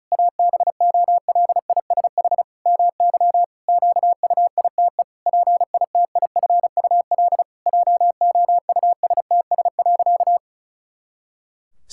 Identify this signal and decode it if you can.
Morse code, heard as a keyed tone, about 36 wpm, 700 Hz. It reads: ABOLISH MY QUITE PITIFUL JOUSTS.
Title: Pangram Challenge v2 - 35wpm